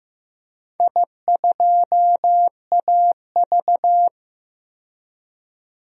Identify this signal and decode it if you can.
Morse code, heard as a keyed tone, about 15 wpm, 700 Hz. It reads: I2AV
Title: DXCC-100 Call Signs - 4 Characters - 15wpm